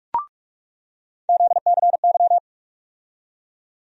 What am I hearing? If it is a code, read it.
Morse code, heard as a keyed tone, about 46 wpm, 700 Hz. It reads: ZCY
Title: USA Call Sign Postfixes - 3 Letters 45wpm